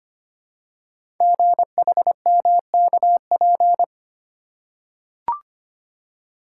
Morse code, heard as a keyed tone, about 25 wpm, 700 Hz. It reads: GHMKP
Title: Code Groups - 5 Letters - 25wpm